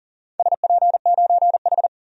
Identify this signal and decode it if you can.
Morse code, heard as a keyed tone, about 39 wpm, 700 Hz. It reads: SP9H
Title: DXCC-100 Call Signs - 4 Characters - 40wpm